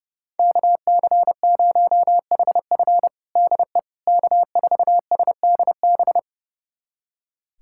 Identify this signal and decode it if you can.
Morse code, heard as a keyed tone, about 30 wpm, 700 Hz. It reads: KC0HF DE K4SDB
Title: USA Callsign DE Callsign - 5 Characters - 30wpm